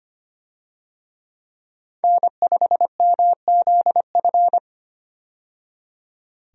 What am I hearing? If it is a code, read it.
Morse code, heard as a keyed tone, about 25 wpm, 700 Hz. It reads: N5MZF
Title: USA Call Signs - 5 Characters - 25wpm